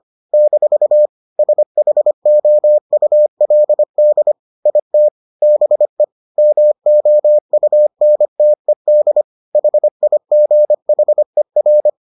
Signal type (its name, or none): Morse code